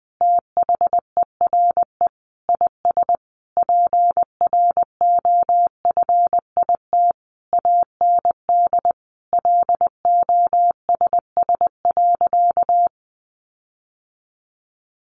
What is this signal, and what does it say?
Morse code, heard as a keyed tone, about 20 wpm, 700 Hz. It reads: THERE IS PROFIT AND LOSS.
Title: Sentences from Top 1400 Words 20wpm